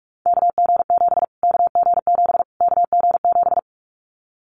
Morse code, heard as a keyed tone, about 44 wpm, 700 Hz. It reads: KG7 KG7 KG7
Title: Callsign Prefixes for Top 25 DXCC - Repeated 3x - 45wpm